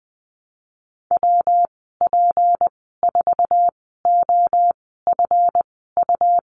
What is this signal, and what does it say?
Morse code, heard as a keyed tone, about 20 wpm, spaced as for 14 wpm, 700 Hz. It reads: WP4OFU
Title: 207 Lesson 33g - Letter j - Call Signs 20wpm